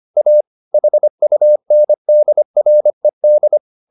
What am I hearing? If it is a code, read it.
Morse code, heard as a keyed tone, about 25 wpm, 600 Hz. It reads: A HUNDRED